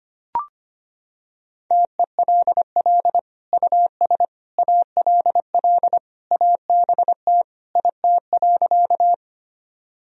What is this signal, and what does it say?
Morse code, heard as a keyed tone, about 25 wpm, 700 Hz. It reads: TELL US ALL ABT IT.
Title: Sentences from Top 200 Words and Abbreviations 25wpm